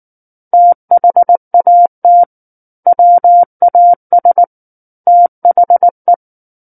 Morse code, heard as a keyed tone, about 19 wpm, 700 Hz. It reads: THAT WAS THE
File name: Book One / 10-Chapter 10 - 19 WPM